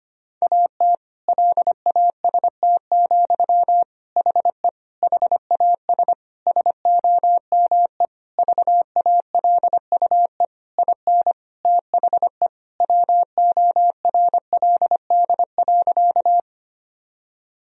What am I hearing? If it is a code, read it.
Morse code, heard as a keyed tone, about 25 wpm, 700 Hz. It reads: AT LAST, HE HAS SOME VALUE IN THE WORLD.
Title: Sentences from Top 1000 Words - Review Part 3 - 25wpm